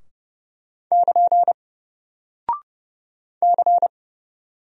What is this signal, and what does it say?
Morse code, heard as a keyed tone, about 30 wpm, 700 Hz. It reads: <KN> C
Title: Single Character (Letter:Number:Punctuation:Prosign) Alt 30wpm